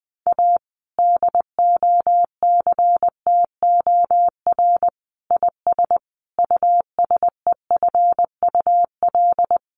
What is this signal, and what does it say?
Morse code, heard as a keyed tone, about 19 wpm, 700 Hz. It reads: A DOCTOR IS USEFUL
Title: Sentences from Top 1500 Words 20wpm